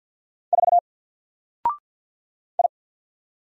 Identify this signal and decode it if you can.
Morse code, heard as a keyed tone, about 48 wpm, 700 Hz. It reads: 4 I
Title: Single Character (Letter:Number:Punctuation:Prosign) 50wpm